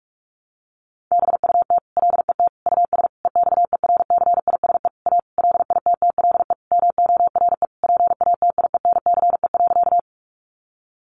Morse code, heard as a keyed tone, about 44 wpm, 700 Hz. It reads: BUT LET US EXERCISE A LITTLE MORE PATIENCE.
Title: Sentences from Top 1300 Words 45wpm